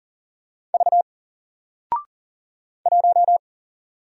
Morse code, heard as a keyed tone, about 40 wpm, 700 Hz. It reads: V 1
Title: Single Letter-Number 40wpm